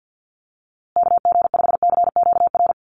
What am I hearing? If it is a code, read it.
Morse code, heard as a keyed tone, about 50 wpm, 700 Hz. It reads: KG5CQR
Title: USA Call Signs Repeated 3x 50wpm